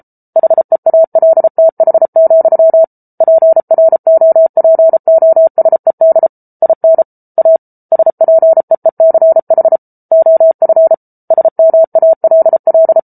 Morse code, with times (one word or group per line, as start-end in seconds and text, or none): 0.4-2.8 HEALTH,
3.2-6.3 PROPOSED
6.6-7.0 IN
7.4-7.6 A
7.9-9.8 SPEECH
10.1-10.9 OF
11.3-13.0 SMALL